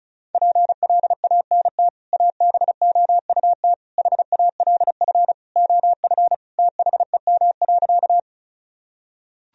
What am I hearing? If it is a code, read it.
Morse code, heard as a keyed tone, about 36 wpm, 700 Hz. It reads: PLANT ABOUT HALF OF THEM.
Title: Sentences from Top 1000 Words - Review Part 1 - 35wpm